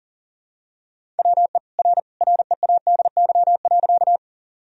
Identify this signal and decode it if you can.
Morse code, heard as a keyed tone, about 40 wpm, 700 Hz. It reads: WE R READY.
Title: Sentences from Top 400 Words and Abbreviations 40wpm